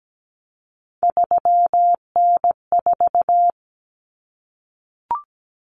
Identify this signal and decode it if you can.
Morse code, heard as a keyed tone, about 17 wpm, 700 Hz. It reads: 3N4
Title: Common Cut Numbers - 3-Digit 17wpm